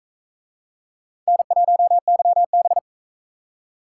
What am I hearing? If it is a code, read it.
Morse code, heard as a keyed tone, about 44 wpm, 700 Hz. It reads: N1YB